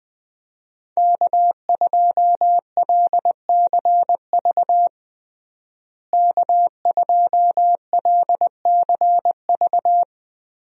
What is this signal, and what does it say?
Morse code, heard as a keyed tone, about 20 wpm, 700 Hz. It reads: K2LCV K2LCV
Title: USA Call Signs - 5 Characters - Repeated 2x - 20wpm